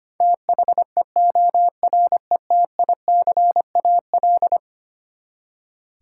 Morse code, heard as a keyed tone, about 25 wpm, 700 Hz. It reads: THEORETICAL
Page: morse-code-ninja-practice-25wpm